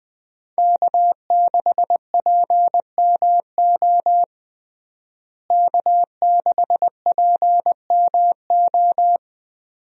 Morse code, heard as a keyed tone, about 20 wpm, 700 Hz. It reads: K6PMO K6PMO
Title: USA Call Signs - 5 Characters - Repeated 2x - 20wpm